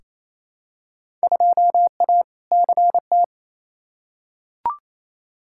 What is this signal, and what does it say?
Morse code, heard as a keyed tone, about 28 wpm, 700 Hz. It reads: 2A CT
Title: Field Day Exchanges Encore - 28wpm